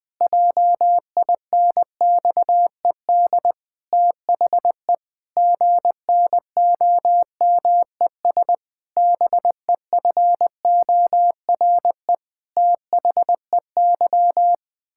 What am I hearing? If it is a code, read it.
Morse code, heard as a keyed tone, about 20 wpm, 700 Hz. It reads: JINXED THE GNOMES BEFORE THEY
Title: Pangram Challenge v2 - 20wpm